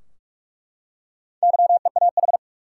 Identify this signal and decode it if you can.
Morse code, heard as a keyed tone, about 46 wpm, 700 Hz. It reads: YEAH
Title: Four Letter Words - 45wpm